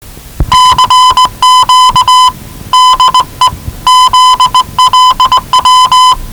Morse code, heard as a keyed tone, about 18 wpm, 1010 Hz. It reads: CQ DE ZLW
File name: WellingtonRadioZlwCloseDownOnHfAndMf / 01-Zlw 4277khz 417.5 qrt until 0925z